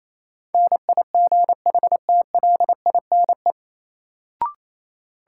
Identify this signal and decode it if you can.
Morse code, heard as a keyed tone, about 28 wpm, 700 Hz. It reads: NIGHTLINE
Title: Sets of 5 Words - New Word Drill - 28wpm